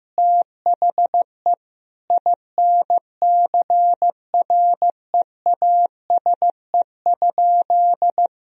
Morse code, heard as a keyed tone, about 15 wpm, 700 Hz. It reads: THE INCREASE?
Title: Sentences from Top 1000 Words - Review Part 4 - 15wpm